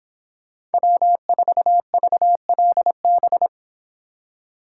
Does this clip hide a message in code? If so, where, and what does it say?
Yes: Morse code from 0.7 to 3.5 s, heard as a keyed tone, about 31 wpm, 700 Hz. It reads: W4VLB